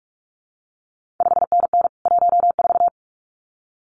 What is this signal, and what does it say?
Morse code, heard as a keyed tone, about 46 wpm, 700 Hz. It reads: 5NN 14
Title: CQ WW DX Contest Exchange - 45wpm